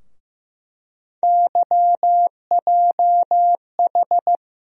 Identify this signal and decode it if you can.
Morse code, heard as a keyed tone, about 15 wpm, 700 Hz. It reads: YJH
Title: USA Call Sign Postfixes - 3 Letters 15wpm